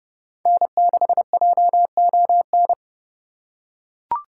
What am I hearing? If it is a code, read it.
Morse code, heard as a keyed tone, about 29 wpm, 700 Hz. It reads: N6JON